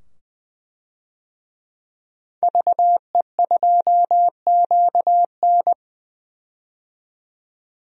Morse code, heard as a keyed tone, about 20 wpm, 700 Hz. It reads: VE2QN
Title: DXCC-100 Call Signs - 5 Characters - 20wpm